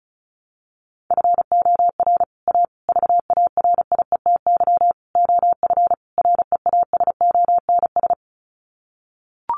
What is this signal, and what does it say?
Morse code, heard as a keyed tone, about 34 wpm, 700 Hz. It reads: FOR A VARIETY OF REASONS
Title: Sets of 5 Words Encore 35wpm